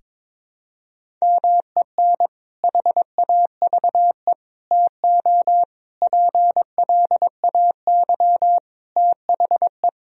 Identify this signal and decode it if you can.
Morse code, heard as a keyed tone, about 21 wpm, 700 Hz. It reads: MEN HAVE TO PLAY THE
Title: Sentences from Top 1000 Words - Review Part 3 - 22wpm